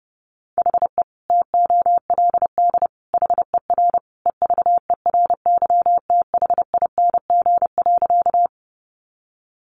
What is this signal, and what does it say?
Morse code, heard as a keyed tone, about 30 wpm, 700 Hz. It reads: HE TOLD HER EVERYTHING.
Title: Sentences from Top 700 Words 30wpm